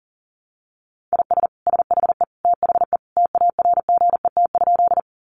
Morse code, heard as a keyed tone, about 40 wpm, 700 Hz. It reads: IS SHE THE TARGET?
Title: Sentences from Top 1200 Words 40wpm